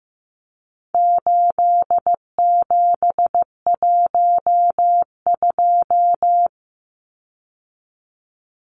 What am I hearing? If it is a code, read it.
Morse code, heard as a keyed tone, about 15 wpm, 700 Hz. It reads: 8712
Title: Numbers - 4-Digit 15wpm